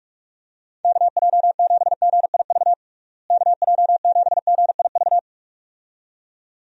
Morse code, heard as a keyed tone, about 46 wpm, 700 Hz. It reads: KJ7GIV KJ7GIV
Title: USA Call Signs - 6 Characters - Repeated 2x - 45wpm